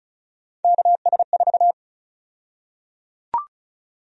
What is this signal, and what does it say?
Morse code, heard as a keyed tone, about 34 wpm, 700 Hz. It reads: KS4
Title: USA Call Sign Prefixes - 35wpm